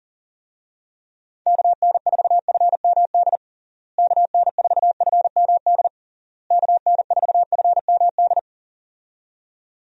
Morse code, heard as a keyed tone, about 40 wpm, 700 Hz. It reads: KN4FMD KN4FMD KN4FMD
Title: USA Call Signs - 6 Characters - Repeated 3x - 40wpm